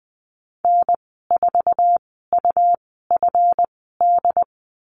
Morse code, heard as a keyed tone, about 19 wpm, spaced as for 14 wpm, 700 Hz. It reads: N4UFD
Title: 167 Lesson 27g - Number 7 - Call Signs 20wpm